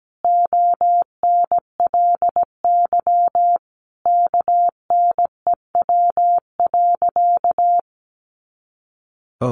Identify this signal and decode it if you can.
Morse code, heard as a keyed tone, about 17 wpm, 700 Hz. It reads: ONLY KNEW.
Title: Sentences from Top 500 Words 17wpm